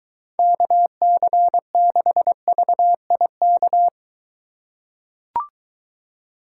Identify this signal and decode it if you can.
Morse code, heard as a keyed tone, about 21 wpm, 700 Hz. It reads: KC6VIK